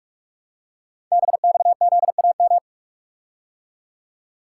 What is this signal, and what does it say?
Morse code, heard as a keyed tone, about 44 wpm, 700 Hz. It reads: BXZAM
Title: Code Groups - 5 Letters - 45wpm